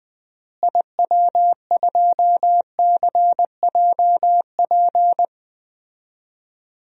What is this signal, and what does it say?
Morse code, heard as a keyed tone, about 21 wpm, 700 Hz. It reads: IW2CJP
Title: DXCC-100 Call Signs - 6 Characters - Repeated 3x - 20wpm